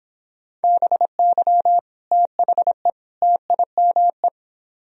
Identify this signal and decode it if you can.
Morse code, heard as a keyed tone, about 31 wpm, 700 Hz. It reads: BY THE TIME